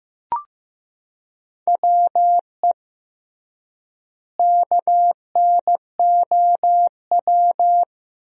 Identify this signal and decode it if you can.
Morse code, heard as a keyed tone, about 15 wpm, 700 Hz. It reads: WE KNOW
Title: 3x Word-Spacing - Sentences from Top 500 Words - 15wpm